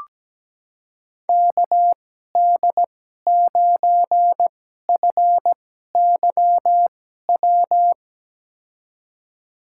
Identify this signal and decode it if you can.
Morse code, heard as a keyed tone, about 17 wpm, spaced as for 12 wpm, 700 Hz. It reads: KD9FYW
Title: USA Call Signs - 6 Characters - 2x Character Spacing - 17wpm